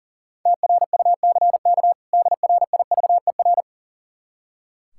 Morse code, heard as a keyed tone, about 40 wpm, 700 Hz. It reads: TRUCK DRIVER
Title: Practice QSO - Exchange 3 - More Info - 40wpm